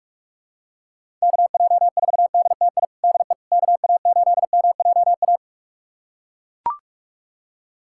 Morse code, heard as a keyed tone, about 42 wpm, 690 Hz. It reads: KJ4DTI DE KA8MJA
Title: USA Callsign DE Callsign 45wpm